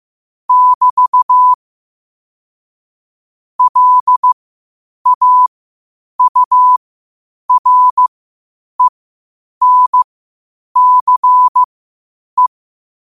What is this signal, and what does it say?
Morse code, heard as a keyed tone, about 15 wpm, spaced as for 8 wpm, 1000 Hz. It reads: <BT> LAURENCE
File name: qotdfiles / 8wpm